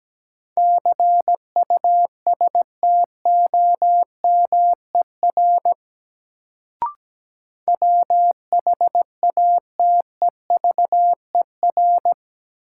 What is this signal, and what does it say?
Morse code, heard as a keyed tone, about 17 wpm, 700 Hz. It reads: CUSTOMER WHATEVER
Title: Sentences from Top 1100 Words - New Word Drill - 17wpm